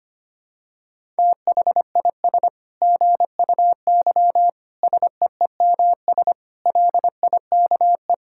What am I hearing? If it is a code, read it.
Morse code, heard as a keyed tone, about 25 wpm, 700 Hz. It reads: THIS GUY SEEMS LIKE
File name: practice / 25wpm / Sentences with 9-Word Phrases - 25wpm